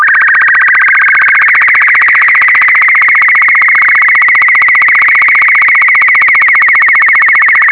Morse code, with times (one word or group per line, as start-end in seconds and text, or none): none